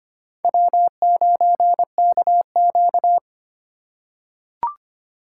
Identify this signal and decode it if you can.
Morse code, heard as a keyed tone, about 25 wpm, 700 Hz. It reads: W9KQ
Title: USA Call Signs - 4 Characters - 2x Character Spacing - 25wpm